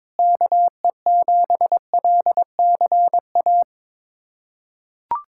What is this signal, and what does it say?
Morse code, heard as a keyed tone, about 22 wpm, 700 Hz. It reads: KE7LCA
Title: USA Callsign DE Callsign - 6 Characters - 22wpm